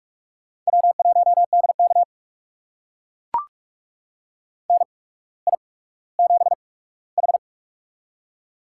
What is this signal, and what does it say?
Morse code, heard as a keyed tone, about 44 wpm, spaced as for 19 wpm, 700 Hz. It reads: W1DK NI7H
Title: USA Call Signs - 8x Character Spacing - 45wpm